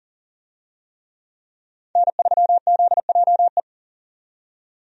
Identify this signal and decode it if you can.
Morse code, heard as a keyed tone, about 40 wpm, 700 Hz. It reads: N3ZJE
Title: USA Call Signs - 5 Characters - 40wpm